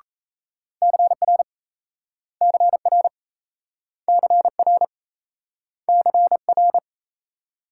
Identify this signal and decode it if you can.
Morse code, heard as a keyed tone, about 38 wpm, 700 Hz. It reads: CR CR CR CR